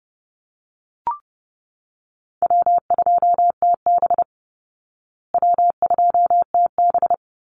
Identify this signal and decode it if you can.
Morse code, heard as a keyed tone, about 30 wpm, 700 Hz. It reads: W2TB W2TB
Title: Easy Call Signs Repeated 2x - 30wpm